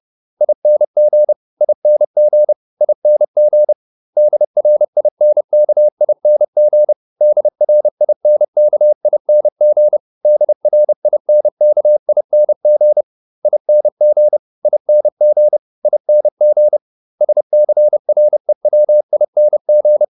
Morse code, heard as a keyed tone, about 30 wpm, 600 Hz. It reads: ING ING ING DRINKING DRINKING DRINKING ING ING ING SCREWING
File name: ING8 30